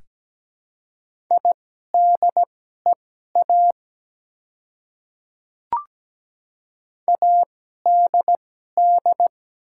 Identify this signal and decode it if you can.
Morse code, heard as a keyed tone, about 17 wpm, spaced as for 12 wpm, 700 Hz. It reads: IDEA ADD
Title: 2x Character Spacing - Top 300 Words - 17wpm